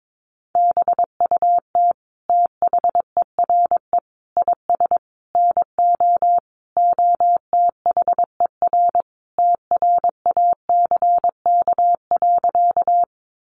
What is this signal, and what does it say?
Morse code, heard as a keyed tone, about 22 wpm, 700 Hz. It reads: BUT THERE IS NO OTHER TRACK.
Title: Sentences from Top 1100 Words 22wpm